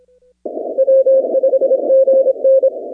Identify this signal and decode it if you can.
Morse code, heard as a keyed tone, about 26 wpm, 500 Hz. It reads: W5GN
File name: 2019-cw-wpx-kt5j-k5pi